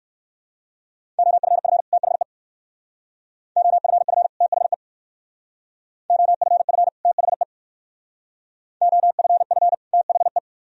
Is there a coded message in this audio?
Yes: Morse code from 1.2 to 10.4 s, heard as a keyed tone, about 57 wpm, 700 Hz. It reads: OFF THE OFF THE OFF THE OFF THE